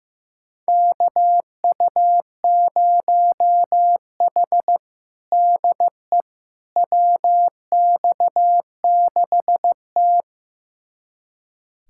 Morse code, heard as a keyed tone, about 16 wpm, 700 Hz. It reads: KU0H DE WX6T
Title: USA Callsign DE Callsign - 4 Characters - 15wpm